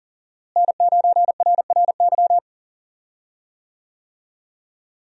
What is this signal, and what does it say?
Morse code, heard as a keyed tone, about 40 wpm, 700 Hz. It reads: N9RRY
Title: USA Call Signs - 5 Characters - Repeated 2x - 40wpm